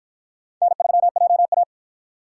Morse code, heard as a keyed tone, about 44 wpm, 700 Hz. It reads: N3JA